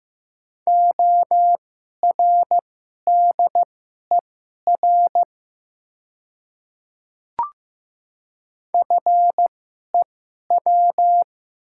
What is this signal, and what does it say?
Morse code, heard as a keyed tone, about 15 wpm, spaced as for 11 wpm, 700 Hz. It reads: ORDER FEW
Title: 2x Character Spacing - Top 500 Words - 15wpm